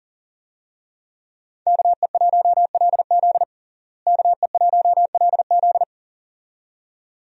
Morse code, heard as a keyed tone, about 40 wpm, 700 Hz. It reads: KE1LZ KE1LZ
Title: USA Call Signs - 5 Characters - Repeated 2x - 40wpm